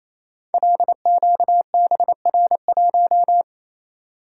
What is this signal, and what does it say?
Morse code, heard as a keyed tone, about 29 wpm, 700 Hz. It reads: LQBR1
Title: Code Groups - 5 Letter:Numbers - 28wpm